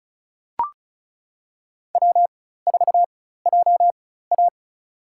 Morse code, heard as a keyed tone, about 34 wpm, spaced as for 16 wpm, 700 Hz. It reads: W4JA
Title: USA Call Signs - 4 Characters - 4x Character Spacing - 35wpm